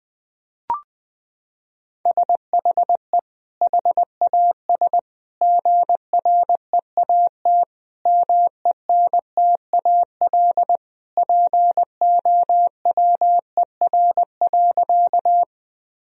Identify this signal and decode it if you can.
Morse code, heard as a keyed tone, about 21 wpm, 700 Hz. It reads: SHE HAS GREAT MENTAL POWER.
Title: Sentences from Top 1000 Words - Review Part 4 - 20wpm